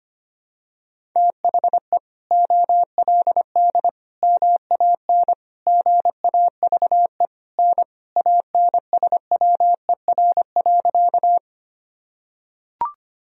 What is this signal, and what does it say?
Morse code, heard as a keyed tone, about 25 wpm, 700 Hz. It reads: THE OLD MAN GAVE N ANSWER.
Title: Sentences from Top 500 Words and Abbreviations 25wpm